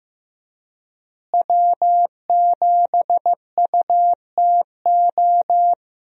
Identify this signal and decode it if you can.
Morse code, heard as a keyed tone, about 15 wpm, 700 Hz. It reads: W7UTO
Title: USA Call Signs - 5 Characters - Repeated 3x - 15wpm